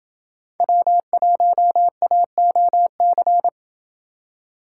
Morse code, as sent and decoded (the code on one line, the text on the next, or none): .-- .---- .- --- -.-.
W1AOC